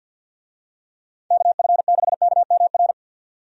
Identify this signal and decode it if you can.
Morse code, heard as a keyed tone, about 52 wpm, 700 Hz. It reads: KF6KMR
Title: USA Call Signs - 6 Characters - 50wpm